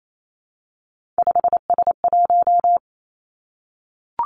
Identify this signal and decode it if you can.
Morse code, heard as a keyed tone, about 28 wpm, 700 Hz. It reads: 5S1
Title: Three Challenging Letter-Number Combinations 28wpm